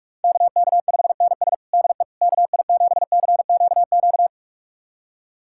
Morse code, heard as a keyed tone, about 41 wpm, 700 Hz. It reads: KK5NS DE KI7CQQ
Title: USA Callsign DE Callsign 45wpm